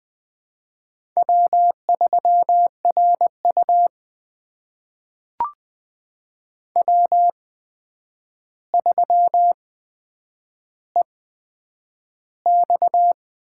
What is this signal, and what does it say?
Morse code, heard as a keyed tone, about 20 wpm, spaced as for 9 wpm, 700 Hz. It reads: W3RU W3EX
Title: USA Call Signs - 8x Character Spacing - 20wpm